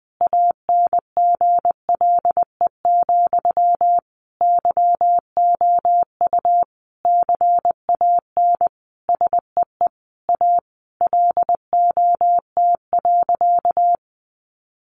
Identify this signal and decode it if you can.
Morse code, heard as a keyed tone, about 20 wpm, 700 Hz. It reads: ANGLE, YOU CAN SEE A LOT.
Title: Sentences from Top 1500 Words 20wpm